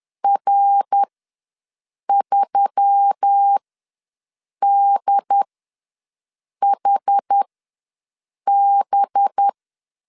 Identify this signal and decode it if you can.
Morse code, heard as a keyed tone, about 10 wpm, spaced as for 6 wpm, 800 Hz. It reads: R3DHB
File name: LEZIONI / 6 PKX0 / 6 P ws 10wpm 3751